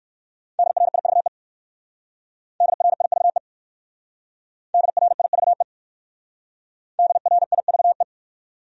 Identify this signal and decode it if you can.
Morse code, heard as a keyed tone, about 56 wpm, 700 Hz. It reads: DRIVE DRIVE DRIVE DRIVE
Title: Speed-Racing Top 500 Words 45wpm